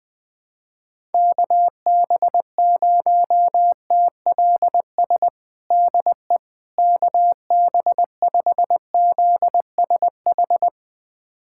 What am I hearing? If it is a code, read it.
Morse code, heard as a keyed tone, about 20 wpm, 700 Hz. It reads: KB0TLS DE KB5ZSH
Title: USA Callsign DE Callsign - 6 Characters - 20wpm